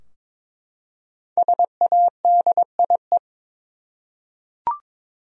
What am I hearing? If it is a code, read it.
Morse code, heard as a keyed tone, about 22 wpm, 700 Hz. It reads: SADIE
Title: USA Common Names - 22wpm